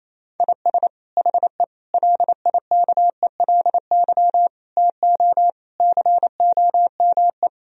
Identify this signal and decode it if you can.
Morse code, heard as a keyed tone, about 28 wpm, 700 Hz. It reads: IS HE LIKELY TO COME
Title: Sentences from Top 1000 Words - Review Part 4 - 28wpm